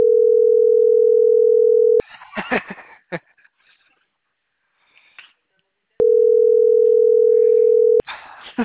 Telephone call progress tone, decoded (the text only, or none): ringback tone